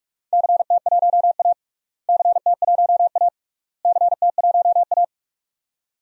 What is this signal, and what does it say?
Morse code, heard as a keyed tone, about 46 wpm, 690 Hz. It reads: CT1A CT1A CT1A
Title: DXCC-100 Call Signs - 4 Characters - Repeated 3x - 45wpm